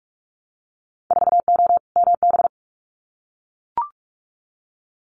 Morse code, heard as a keyed tone, about 44 wpm, 700 Hz. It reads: 4O MB
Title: Winter Field Day Exchanges Encore 45wpm